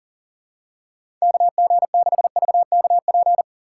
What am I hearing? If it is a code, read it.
Morse code, heard as a keyed tone, about 40 wpm, 700 Hz. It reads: KG6VKP
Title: USA Call Signs - 3x Character Spacing - 40wpm